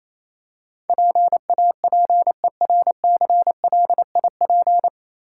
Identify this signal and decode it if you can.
Morse code, heard as a keyed tone, about 27 wpm, 690 Hz. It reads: PAPERCLIP
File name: Easy Words - Speed-Racing with Hints - 22wpm